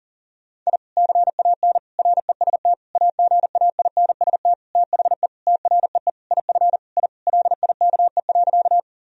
Morse code, heard as a keyed tone, about 40 wpm, 700 Hz. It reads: I CAN REST AGAINST THE TREE IF I LIKE.
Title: Sentences from Top 1000 Words - Review Part 1 - 40wpm